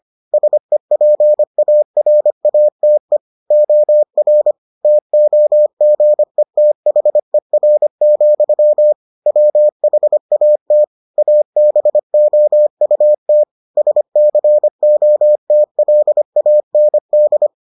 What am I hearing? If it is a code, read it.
Morse code, heard as a keyed tone, about 25 wpm, 600 Hz. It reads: SEPARATE OR TOGETHER, WHAT ABOUT SCOTLAND